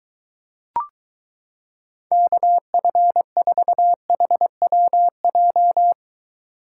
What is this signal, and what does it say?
Morse code, heard as a keyed tone, about 22 wpm, 700 Hz. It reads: KF4HWJ